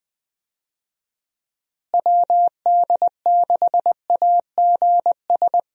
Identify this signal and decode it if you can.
Morse code, heard as a keyed tone, about 20 wpm, 700 Hz. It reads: WD6AGS
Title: USA Call Signs - 6 Characters - Repeated 3x - 20wpm